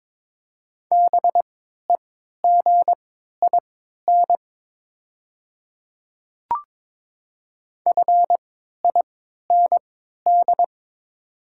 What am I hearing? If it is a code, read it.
Morse code, heard as a keyed tone, about 22 wpm, spaced as for 12 wpm, 690 Hz. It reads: BEGIN FIND
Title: 3x Character Spacing - Top 300 Words - 22wpm